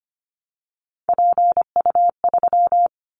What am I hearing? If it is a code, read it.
Morse code, heard as a keyed tone, about 25 wpm, 700 Hz. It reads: PU3
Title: Callsign Prefixes for Top 50 DXCC - 25wpm